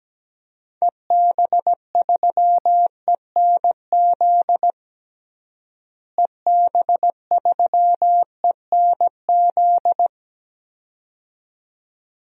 Morse code, heard as a keyed tone, about 17 wpm, 700 Hz. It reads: EB3ENZ EB3ENZ
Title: DXCC-100 Call Signs - 6 Characters - Repeated 2x - 17wpm